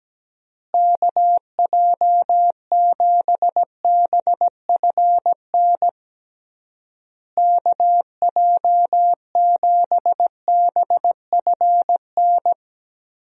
Morse code, heard as a keyed tone, about 17 wpm, 700 Hz. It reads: KJ7BFN KJ7BFN
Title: USA Call Signs - 6 Characters - Repeated 3x - 17wpm